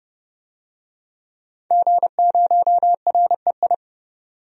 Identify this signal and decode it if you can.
Morse code, heard as a keyed tone, about 29 wpm, 690 Hz. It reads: G0REI
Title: DXCC-100 Call Signs - 5 Characters - 30wpm